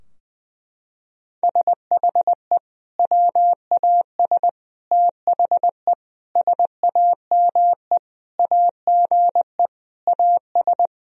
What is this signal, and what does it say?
Morse code, heard as a keyed tone, about 20 wpm, 700 Hz. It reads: SHE WAS THE SAME AGE AS
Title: Sentences from Top 1000 Words - Review Part 3 - 20wpm